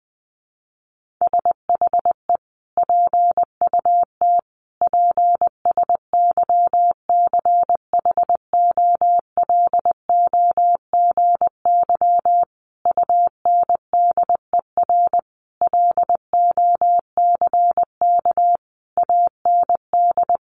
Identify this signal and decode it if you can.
Morse code, heard as a keyed tone, about 20 wpm, 700 Hz. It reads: SHE PUT PSYCHOLOGY UNDER LOCK AND
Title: Sentences from Top 1400 Words 20wpm